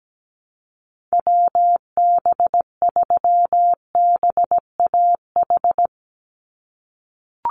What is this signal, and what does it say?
Morse code, heard as a keyed tone, about 17 wpm, 700 Hz. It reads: WB3BAH
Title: USA Call Signs 17wpm